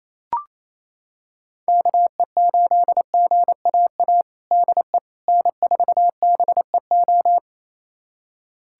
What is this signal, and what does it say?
Morse code, heard as a keyed tone, about 28 wpm, 700 Hz. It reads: KE8GAA DE N4BEO
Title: USA Callsign DE Callsign 28wpm